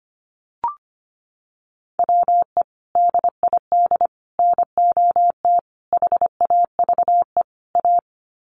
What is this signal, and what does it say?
Morse code, heard as a keyed tone, about 25 wpm, 700 Hz. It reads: WE DID NOT HAVE A
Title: Sets of 5 Words Encore2 25wpm